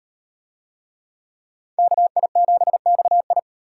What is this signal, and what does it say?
Morse code, heard as a keyed tone, about 35 wpm, 700 Hz. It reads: KI7XI